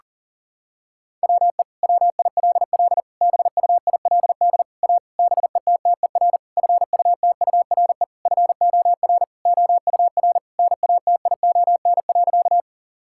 Morse code, heard as a keyed tone, about 40 wpm, 700 Hz. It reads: WE WILL BUILD A BETTER FUTURE FOR OUR NATION.
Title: Sentences from Top 1000 Words - Review Part 6 - 40wpm